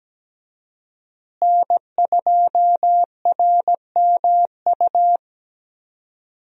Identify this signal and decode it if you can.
Morse code, heard as a keyed tone, about 17 wpm, 700 Hz. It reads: N2RMU
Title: USA Call Signs - 5 Characters - Repeated 2x - 17wpm